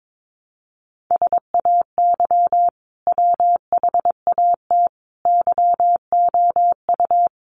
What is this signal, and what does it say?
Morse code, heard as a keyed tone, about 22 wpm, 700 Hz. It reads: SAY WHAT YOU
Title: Sentences from Top 200 Words 22wpm